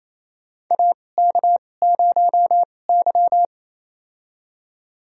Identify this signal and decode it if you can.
Morse code, heard as a keyed tone, about 28 wpm, spaced as for 20 wpm, 700 Hz. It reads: AK0Y
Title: USA Call Signs - 4 Characters - 2x Character Spacing - 28wpm